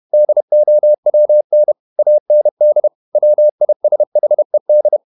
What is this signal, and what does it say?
Morse code, heard as a keyed tone, about 31 wpm, 600 Hz. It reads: DOWN AND WISHED
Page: Never 31wpm | CW med Gnister